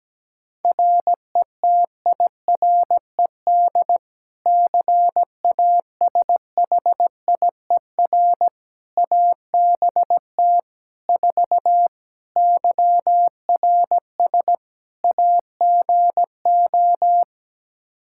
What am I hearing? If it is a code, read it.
Morse code, heard as a keyed tone, about 17 wpm, 700 Hz. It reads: RETIRED CASHIER ABT 4 YRS AGO
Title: Practice QSO - Exchange 3 - More Info - 17wpm